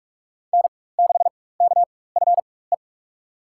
Morse code, heard as a keyed tone, about 42 wpm, spaced as for 20 wpm, 700 Hz. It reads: N6KFE
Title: USA Call Signs - 5 Characters - 4x Character Spacing - 45wpm